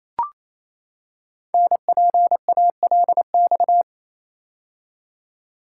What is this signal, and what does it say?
Morse code, heard as a keyed tone, about 27 wpm, 700 Hz. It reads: NPALX
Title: Code Groups - 5 Letter:Numbers - 28wpm